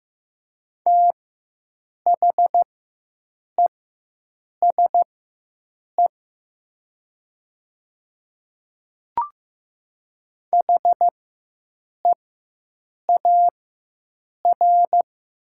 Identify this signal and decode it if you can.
Morse code, heard as a keyed tone, about 15 wpm, spaced as for 7 wpm, 700 Hz. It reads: THESE HEAR
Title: 4x Character Spacing - Top 300 Words - 15wpm